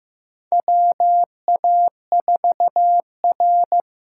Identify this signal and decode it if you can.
Morse code, heard as a keyed tone, about 15 wpm, 700 Hz. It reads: WA4R
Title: USA Call Signs - 6 Characters - 6x Character Spacing - 15wpm